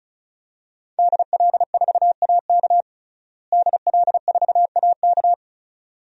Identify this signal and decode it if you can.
Morse code, heard as a keyed tone, about 36 wpm, 700 Hz. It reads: DL4AK DL4AK
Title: DXCC-100 Call Signs - 5 Characters - Repeated 2x - 35wpm